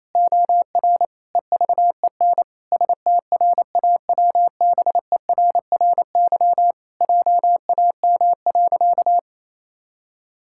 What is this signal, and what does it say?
Morse code, heard as a keyed tone, about 27 wpm, 700 Hz. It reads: OR EVEN STRAWBERRY JAM.
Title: Pangram Challenge v2 - 28wpm